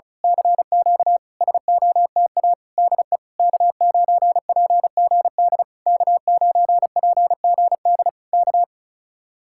Morse code, heard as a keyed tone, about 32 wpm, 700 Hz. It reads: CQ SOTA DE K9PGD K9PGD K
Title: Speed-Racing Common QSO Phrases Encore - 35wpm